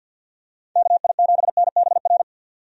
Morse code, heard as a keyed tone, about 46 wpm, 700 Hz. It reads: KI7NBR
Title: USA Call Signs - 6 Characters - 50wpm